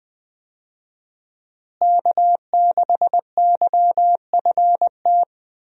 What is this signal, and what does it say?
Morse code, heard as a keyed tone, about 20 wpm, 700 Hz. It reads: K6YFT
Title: USA Call Signs - 5 Characters - Repeated 3x - 20wpm